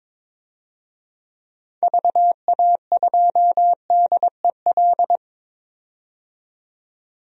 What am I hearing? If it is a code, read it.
Morse code, heard as a keyed tone, about 22 wpm, 700 Hz. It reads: VA2DEL
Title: DXCC-100 Call Signs - 6 Characters - 22wpm